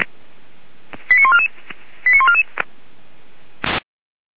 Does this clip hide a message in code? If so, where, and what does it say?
Yes: five-tone selective call (ZVEI) from 1.1 to 2.4 s, heard as quick runs of five tones, tones of about 70 ms. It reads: 89144 89144